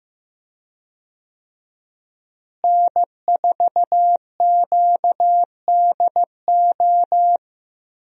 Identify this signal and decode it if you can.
Morse code, heard as a keyed tone, about 15 wpm, 700 Hz. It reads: N4QDO
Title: USA Call Signs - 5 Characters - Repeated 2x - 15wpm